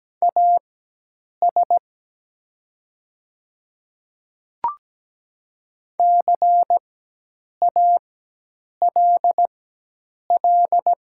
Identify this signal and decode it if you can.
Morse code, heard as a keyed tone, about 17 wpm, spaced as for 8 wpm, 700 Hz. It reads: AS CALL
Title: 4x Character Spacing - Top 300 Words - 17wpm